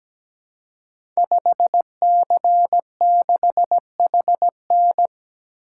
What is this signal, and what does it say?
Morse code, heard as a keyed tone, about 17 wpm, 700 Hz. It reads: 5C6HN
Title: Code Groups - 5 Letter:Numbers - 17wpm